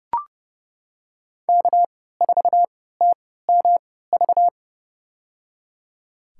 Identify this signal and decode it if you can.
Morse code, heard as a keyed tone, about 30 wpm, spaced as for 17 wpm, 700 Hz. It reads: K4TMV
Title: USA Call Signs - 5 Characters - 3x Character Spacing - 30wpm